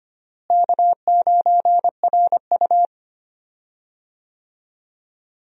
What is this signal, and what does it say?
Morse code, heard as a keyed tone, about 25 wpm, 700 Hz. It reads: K9RU
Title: USA Call Signs - 4 Characters - Repeated 2x - 25wpm